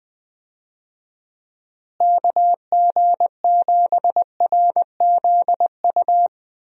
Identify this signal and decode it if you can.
Morse code, heard as a keyed tone, about 20 wpm, 700 Hz. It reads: KG7RZU
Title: USA Call Signs - 6 Characters - Repeated 3x - 20wpm